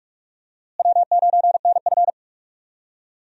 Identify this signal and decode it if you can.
Morse code, heard as a keyed tone, about 45 wpm, 700 Hz. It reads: W9NF
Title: USA Call Signs - 4 Characters - Repeated 2x - 45wpm